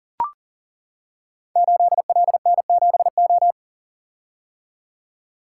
Morse code, heard as a keyed tone, about 38 wpm, 700 Hz. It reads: 8LN7O
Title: Code Groups - 5 Letter:Numbers - 40wpm